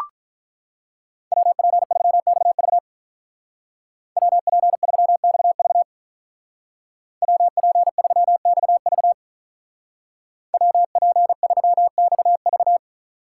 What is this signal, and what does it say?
Morse code, heard as a keyed tone, about 44 wpm, 700 Hz. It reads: WP3XV WP3XV WP3XV WP3XV
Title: Speed Racing - USA 2x2 Call Signs 35wpm